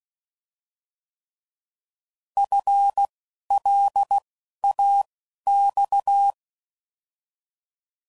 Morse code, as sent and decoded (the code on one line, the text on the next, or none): ..-. / .-.. / .- / -..-
F L A X